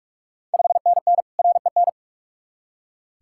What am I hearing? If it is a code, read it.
Morse code, heard as a keyed tone, about 46 wpm, 690 Hz. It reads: 5NN REN
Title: Ontario QSO Party Exchange 45wpm